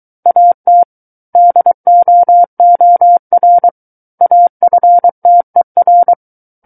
Morse code, heard as a keyed tone, about 23 wpm, 700 Hz. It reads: AT DOOR AFTER
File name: Book One / 14-Chapter 14 - 23 WPM